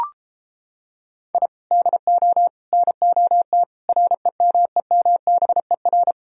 Morse code, heard as a keyed tone, about 29 wpm, 700 Hz. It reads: I DO NOT REMEMBER